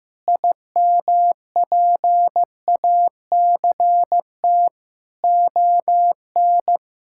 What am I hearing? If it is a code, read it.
Morse code, heard as a keyed tone, about 15 wpm, 700 Hz. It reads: IMPACT ON
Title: Sentences from Top 1000 Words - Review Part 6 - 15wpm